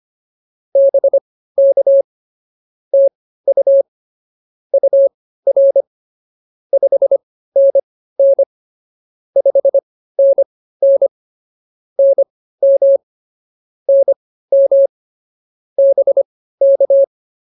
Morse code, heard as a keyed tone, about 24 wpm, spaced as for 15 wpm, 550 Hz. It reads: BK TU UR 5NN 5NN NM NM BK